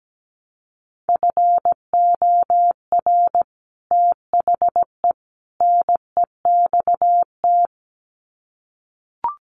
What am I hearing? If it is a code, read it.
Morse code, heard as a keyed tone, about 17 wpm, 700 Hz. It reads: FOR THE NEXT
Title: Sets of 3 Words 17wpm